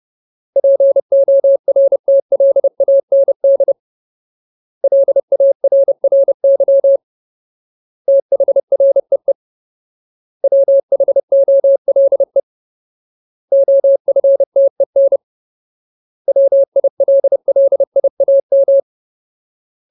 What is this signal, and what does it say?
Morse code, heard as a keyed tone, about 30 wpm, 550 Hz. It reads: PORTLAND LARRY THREE WHOLE OFTEN WILLIAM
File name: Ultimate Word List / 30WPM Ultimate